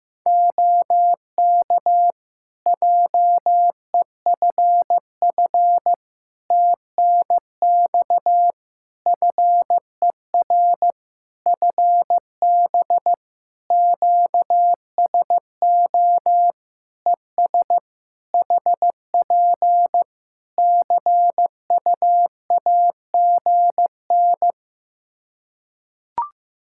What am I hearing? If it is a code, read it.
Morse code, heard as a keyed tone, about 15 wpm, 700 Hz. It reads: OK JEFF TNX FER FB QSO ES HP CUAGN
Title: Practice QSO - Exchange 4 - Ending - 15wpm